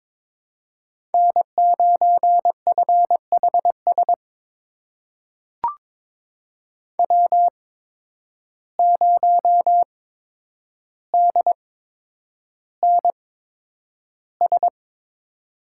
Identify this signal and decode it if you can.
Morse code, heard as a keyed tone, about 22 wpm, spaced as for 9 wpm, 700 Hz. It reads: N9FHS W0DNS
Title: USA Call Signs - 5 Characters - 8x Character Spacing - 22wpm